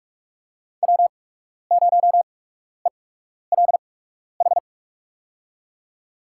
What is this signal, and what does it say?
Morse code, heard as a keyed tone, about 45 wpm, spaced as for 12 wpm, 700 Hz. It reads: W0ELH
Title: USA Call Signs - 5 Characters - 8x Character Spacing - 45wpm